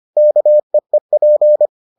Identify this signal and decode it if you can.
Morse code, heard as a keyed tone, about 25 wpm, 600 Hz. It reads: KEEP